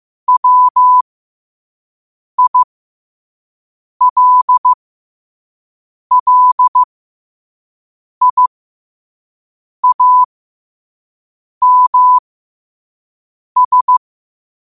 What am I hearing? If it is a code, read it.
Morse code, heard as a keyed tone, about 15 wpm, spaced as for 5 wpm, 1000 Hz. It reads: WILLIAMS
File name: qotdfiles / 5wpm